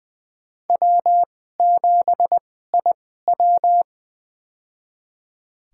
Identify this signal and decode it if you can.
Morse code, heard as a keyed tone, about 20 wpm, spaced as for 14 wpm, 700 Hz. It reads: W7IW
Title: USA Call Signs - 4 Characters - 2x Character Spacing - 20wpm